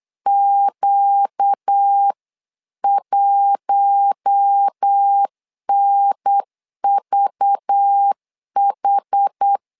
Morse code, heard as a keyed tone, about 8 wpm, 800 Hz. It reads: Q1NVH
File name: LEZIONI / 5 LBVC / 5 LBV 8wpm 2837